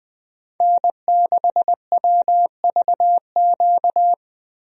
Morse code, heard as a keyed tone, about 21 wpm, 700 Hz. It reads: N6WVQ
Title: USA Call Signs - 5 Characters - Repeated 3x - 20wpm